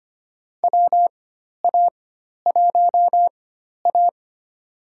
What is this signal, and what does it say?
Morse code, heard as a keyed tone, about 25 wpm, spaced as for 12 wpm, 700 Hz. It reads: WA1A
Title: USA Call Signs - 4 Characters - 4x Character Spacing - 25wpm